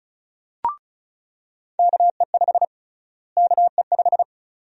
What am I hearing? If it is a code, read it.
Morse code, heard as a keyed tone, about 36 wpm, 700 Hz. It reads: KE5 KE5
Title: USA Call Sign Prefixes - Repeated 2x - 35wpm